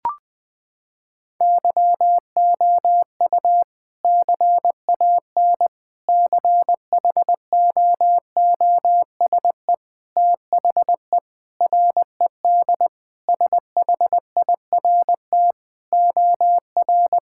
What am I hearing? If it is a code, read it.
Morse code, heard as a keyed tone, about 20 wpm, 700 Hz. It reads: YOU CAN CHOOSE THE RED SHIRT OR
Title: Sentences with 2-Word Phrases - 20wpm